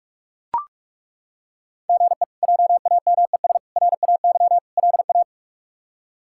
Morse code, heard as a keyed tone, about 41 wpm, 700 Hz. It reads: GE JAMES RAY LA
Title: K1USN SST QSO Practice - 45wpm